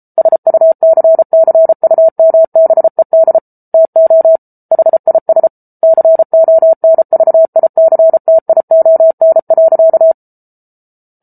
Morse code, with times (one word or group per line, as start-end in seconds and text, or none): 0.2-3.4 SUCCUMBED
3.7-4.4 TO
4.7-5.5 HIS
5.8-10.1 CONVICTION.